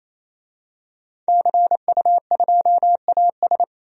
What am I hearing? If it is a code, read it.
Morse code, heard as a keyed tone, about 28 wpm, 700 Hz. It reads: CU2AS
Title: DXCC-100 Call Signs - 5 Characters - 28wpm